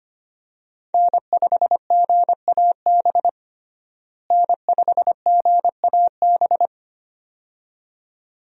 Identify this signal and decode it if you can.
Morse code, heard as a keyed tone, about 25 wpm, 700 Hz. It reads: N5GAB N5GAB
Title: USA Call Signs - 5 Characters - Repeated 2x - 25wpm